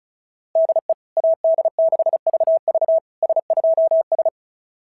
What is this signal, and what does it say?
Morse code, heard as a keyed tone, about 33 wpm, 660 Hz. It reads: DE AD6VV S2S
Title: SOTA QSO Practice 35wpm